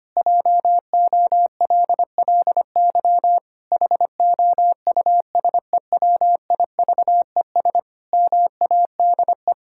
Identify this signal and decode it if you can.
Morse code, heard as a keyed tone, about 25 wpm, 700 Hz. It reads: JOLLY HOUSEWIVES MADE
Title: Pangram Challenge v2 - 25wpm